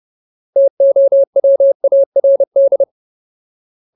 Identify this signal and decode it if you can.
Morse code, heard as a keyed tone, about 30 wpm, 550 Hz. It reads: TOWARD
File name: Ultimate Word List / 30WPM Ultimate